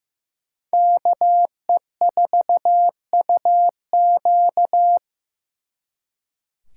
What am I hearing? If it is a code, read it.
Morse code, heard as a keyed tone, about 18 wpm, 700 Hz. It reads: KE4UQ
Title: Speed Racing - USA Call Signs 15wpm